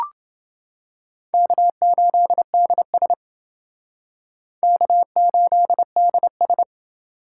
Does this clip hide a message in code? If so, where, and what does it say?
Yes: Morse code from 1.3 to 6.6 s, heard as a keyed tone, about 28 wpm, 700 Hz. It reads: K8DS K8DS